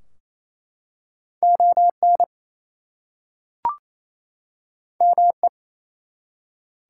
Morse code, heard as a keyed tone, about 27 wpm, 700 Hz. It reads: ON ME
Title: Two Letter Words - 28wpm